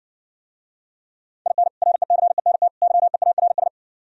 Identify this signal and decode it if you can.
Morse code, heard as a keyed tone, about 60 wpm, 700 Hz. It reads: IN RECENT YEARS